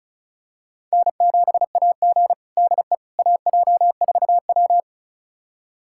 Morse code, heard as a keyed tone, about 36 wpm, 700 Hz. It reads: N7AG DE AJ4W
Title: USA Callsign DE Callsign - 4 Characters - 35wpm